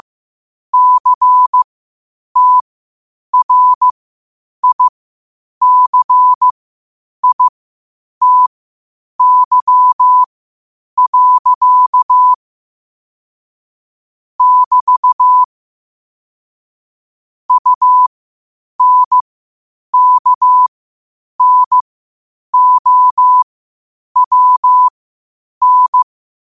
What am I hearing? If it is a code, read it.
Morse code, heard as a keyed tone, about 15 wpm, spaced as for 8 wpm, 1000 Hz. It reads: CTRICITY. <BT> UNKNOWN